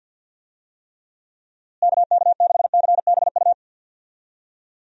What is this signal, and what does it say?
Morse code, heard as a keyed tone, about 51 wpm, 700 Hz. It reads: KK6CBU
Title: USA Call Signs - 6 Characters - Repeated 3x - 50wpm